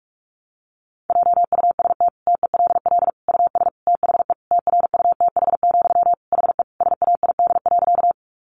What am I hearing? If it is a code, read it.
Morse code, heard as a keyed tone, about 44 wpm, 700 Hz. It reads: JUST TELL US THE TRUTH, HE SAID.
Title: Sentences from Top 800 Words 45wpm